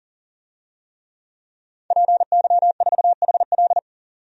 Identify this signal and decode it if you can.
Morse code, heard as a keyed tone, about 40 wpm, 700 Hz. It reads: PY4HL
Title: DXCC-100 Call Signs - 5 Characters - 40wpm